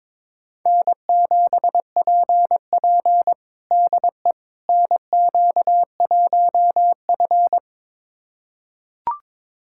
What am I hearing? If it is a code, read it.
Morse code, heard as a keyed tone, about 21 wpm, 700 Hz. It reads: N7PP DE NQ1F
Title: USA Callsign DE Callsign - 4 Characters - 22wpm